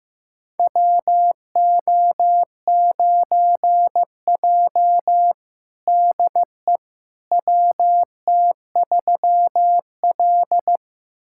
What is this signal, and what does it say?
Morse code, heard as a keyed tone, about 15 wpm, 700 Hz. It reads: WO9J DE WT3L
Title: USA Callsign DE Callsign - 4 Characters - 15wpm